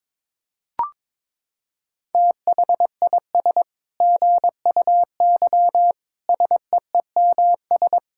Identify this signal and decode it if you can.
Morse code, heard as a keyed tone, about 22 wpm, 700 Hz. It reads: THIS GUY SEEMS
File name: Sentences with 9-Word Phrases - 22wpm